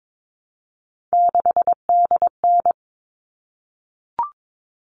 Morse code, heard as a keyed tone, about 22 wpm, 700 Hz. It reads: 6DN
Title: All Cut Numbers - 3-Digit 22wpm